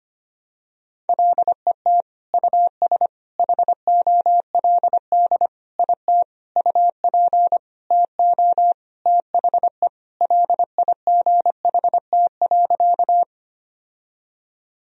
Morse code, heard as a keyed tone, about 25 wpm, 700 Hz. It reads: LET US HOLD IT UP TO THE LIGHT.
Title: Sentences from Top 1000 Words - Review Part 2 - 25wpm